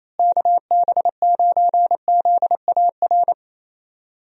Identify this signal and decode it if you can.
Morse code, heard as a keyed tone, about 28 wpm, 700 Hz. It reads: KB9ZAR
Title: USA Call Signs - 6 Characters - Repeated 3x - 28wpm